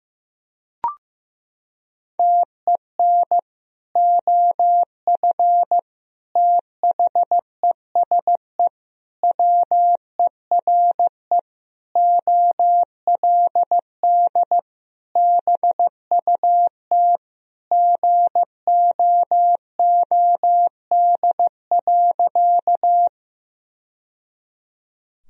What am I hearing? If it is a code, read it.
Morse code, heard as a keyed tone, about 15 wpm, 700 Hz. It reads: TEN OF THESE WERE OLD BUT GOOD.
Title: Sentences from Top 1000 Words - Review Part 2 - 15wpm